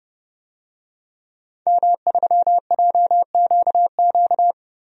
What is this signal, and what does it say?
Morse code, heard as a keyed tone, about 30 wpm, 700 Hz. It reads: M3JQQ
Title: DXCC-100 Call Signs - 5 Characters - Repeated 3x - 30wpm